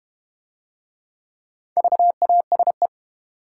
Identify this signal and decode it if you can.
Morse code, heard as a keyed tone, about 32 wpm, 700 Hz. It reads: VASE